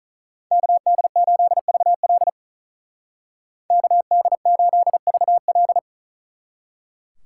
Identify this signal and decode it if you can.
Morse code, heard as a keyed tone, about 45 wpm, 700 Hz. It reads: KD8VL KD8VL